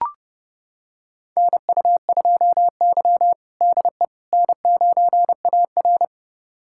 Morse code, heard as a keyed tone, about 30 wpm, 700 Hz. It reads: NU2Y DE N9AR
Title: USA Callsign DE Callsign - 4 Characters - 30wpm